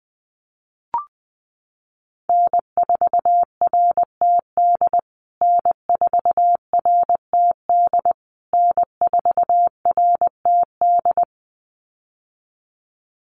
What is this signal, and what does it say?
Morse code, heard as a keyed tone, about 20 wpm, 700 Hz. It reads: N4RTD N4RTD N4RTD
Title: USA Call Signs Repeated 3x 20wpm